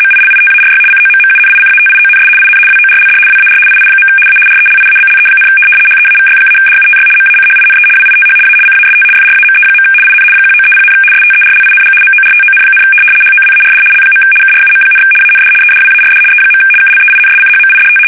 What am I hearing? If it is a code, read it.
RTTY, about 75 baud. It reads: MILITARY COMMUNICATIONS IN WORLD WAR II. A RELIABLE, HEAVY-DUTY MACHINE WITH A CAST FRAME. IN 1930, STERLING MORTON, HOWARD L. KRUM, AND EDWARD E. KLEINSCHMIDT
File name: fsk850-75B / teletype-corp